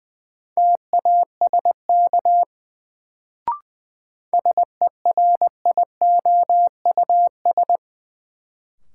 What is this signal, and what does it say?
Morse code, heard as a keyed tone, about 20 wpm, 700 Hz. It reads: TASK SERIOUS
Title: Sentences from Top 1100 Words - New Word Drill Alt - 20wpm